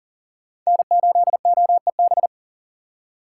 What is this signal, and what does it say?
Morse code, heard as a keyed tone, about 40 wpm, 700 Hz. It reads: N8OEB
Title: USA Call Signs - 5 Characters - 2x Character Spacing - 40wpm